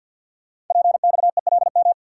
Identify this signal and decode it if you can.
Morse code, heard as a keyed tone, about 52 wpm, 700 Hz. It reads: PXELM
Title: Code Groups - 5 Letters - 50wpm